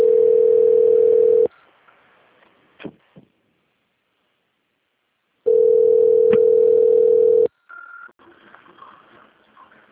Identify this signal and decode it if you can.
Ringback tone (North America), 440 + 480 Hz.
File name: out- - - - - .16904